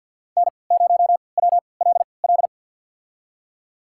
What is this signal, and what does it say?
Morse code, heard as a keyed tone, about 50 wpm, spaced as for 28 wpm, 700 Hz. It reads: N0WLL
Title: USA Call Signs - 3x Character Spacing - 50wpm